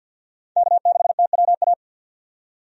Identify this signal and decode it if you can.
Morse code, heard as a keyed tone, about 48 wpm, 700 Hz. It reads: K6TWA
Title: USA Call Signs - 5 Characters - 50wpm